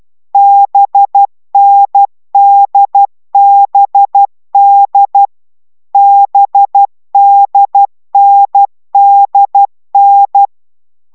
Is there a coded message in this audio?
Yes: Morse code from 0.3 to 10.5 s, heard as a keyed tone, about 12 wpm, 800 Hz. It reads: BNDBD BDNDN